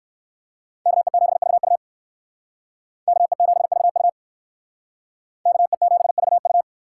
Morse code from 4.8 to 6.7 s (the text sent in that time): KE7VU